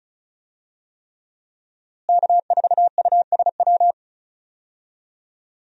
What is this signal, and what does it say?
Morse code, heard as a keyed tone, about 36 wpm, 700 Hz. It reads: K4USW
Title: USA Call Signs - 5 Characters - Repeated 3x - 35wpm